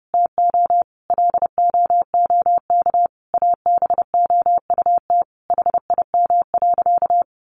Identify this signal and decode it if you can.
Morse code, heard as a keyed tone, about 30 wpm, 700 Hz. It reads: TO LOOK ABOUT HIM.
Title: Sentences from Top 200 Words 30wpm